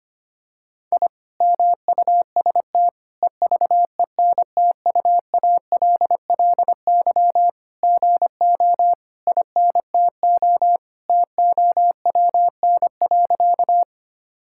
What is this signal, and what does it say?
Morse code, heard as a keyed tone, about 25 wpm, 700 Hz. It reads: I MUST EVENTUALLY GO INTO TOWN.
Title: Sentences from Top 1000 Words - Review Part 6 - 25wpm